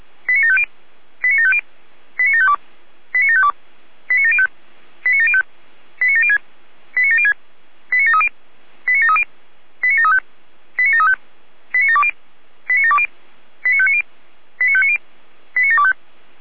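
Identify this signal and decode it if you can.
Five-tone selective call (ZVEI), heard as quick runs of five tones, tones of about 70 ms. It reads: 89750 89750 89752 89752 89705 89705 89706 89706 89730 89730 89735 89735 89720 89720 89590 89590 89725
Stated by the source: ZVEI five-tone selective call received on an SDR.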